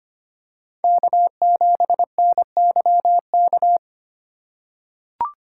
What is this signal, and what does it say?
Morse code, heard as a keyed tone, about 25 wpm, 700 Hz. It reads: K7NYK
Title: USA Call Signs - 2x Character Spacing - 25wpm